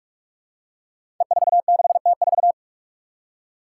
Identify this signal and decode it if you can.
Morse code, heard as a keyed tone, about 42 wpm, 700 Hz. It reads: E46T4
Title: Five Challenging Letter-Number Combinations 45wpm